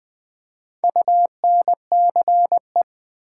Morse code, heard as a keyed tone, about 20 wpm, 700 Hz. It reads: UNCE
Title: Build A Word - 4 Letters At A Time - 11-Letter Words - 20wpm